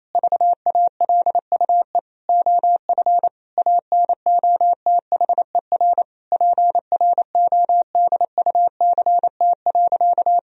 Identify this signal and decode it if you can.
Morse code, heard as a keyed tone, about 28 wpm, 700 Hz. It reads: VALUE OF ANOTHER PRODUCT.
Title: Sentences from Top 1000 Words - Review Part 3 - 28wpm